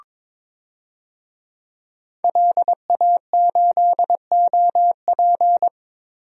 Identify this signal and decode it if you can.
Morse code, heard as a keyed tone, about 22 wpm, 700 Hz. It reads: LA8OP
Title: DXCC-100 Call Signs - 5 Characters - Repeated 3x - 22wpm